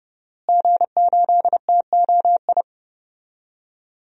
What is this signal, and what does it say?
Morse code, heard as a keyed tone, about 30 wpm, 700 Hz. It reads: G8TOI
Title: DXCC-100 Call Signs - 5 Characters - Repeated 3x - 30wpm